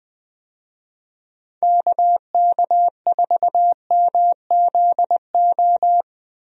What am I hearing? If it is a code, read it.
Morse code, heard as a keyed tone, about 20 wpm, 700 Hz. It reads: KK4MZO
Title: USA Call Signs - 6 Characters - 20wpm